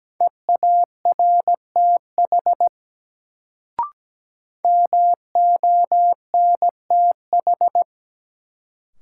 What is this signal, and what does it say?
Morse code, heard as a keyed tone, about 17 wpm, 700 Hz. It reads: EARTH MONTH
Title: Sets of 5 Words Encore - New Word Drill - 17wpm